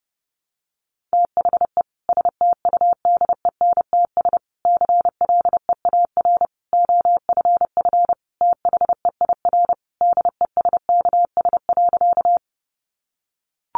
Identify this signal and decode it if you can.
Morse code, heard as a keyed tone, about 30 wpm, 700 Hz. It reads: THE STUDENTS CLEAR OFF THEIR DESKS.
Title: Additional Easy Phrases 30wpm